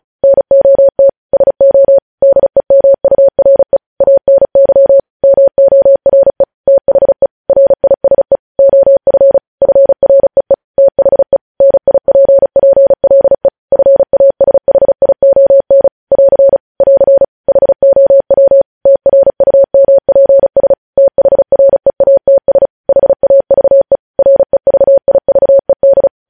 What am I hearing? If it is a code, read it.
Morse code, heard as a keyed tone, about 36 wpm, 550 Hz. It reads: NOT SO DEMURE ANY MORE THE RISE OF FREE THE NIPPLE FASHION <AR> <AR> HOW TRUMPS THREATS HAVE REVIVED
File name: podcast / News headlines-1x-35x35